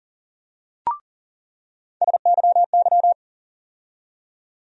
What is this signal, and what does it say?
Morse code, heard as a keyed tone, about 40 wpm, 700 Hz. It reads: SYY
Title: USA Call Sign Postfixes - 3 Letters 40wpm